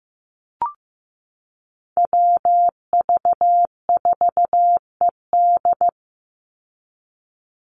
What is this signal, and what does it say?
Morse code, heard as a keyed tone, about 15 wpm, 700 Hz. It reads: WV4ED
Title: USA Call Signs 15wpm